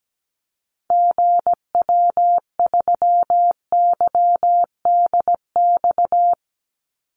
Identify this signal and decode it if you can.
Morse code, heard as a keyed tone, about 17 wpm, 700 Hz. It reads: GW3YDX
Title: Call Signs - 17wpm